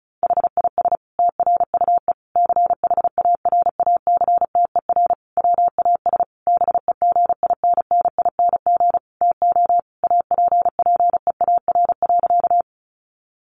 Morse code, heard as a keyed tone, about 34 wpm, 700 Hz. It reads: HIS TRUE CHARACTER WAS BEGINNING TO APPEAR.
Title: Sentences from Top 1200 Words 35wpm